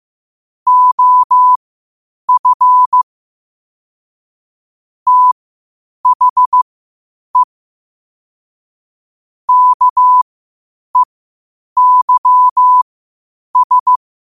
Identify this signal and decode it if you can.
Morse code, heard as a keyed tone, about 15 wpm, spaced as for 8 wpm, 1000 Hz. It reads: OF THE KEYS
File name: qotdfiles / 8wpm